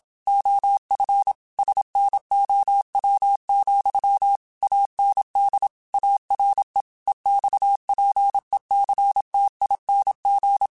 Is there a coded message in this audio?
Yes: Morse code from 0.3 to 10.7 s, heard as a keyed tone, about 27 wpm, 800 Hz. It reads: OF SNOW, AND ARE EXPECTING